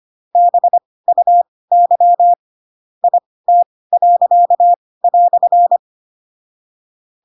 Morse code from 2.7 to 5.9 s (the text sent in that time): IT."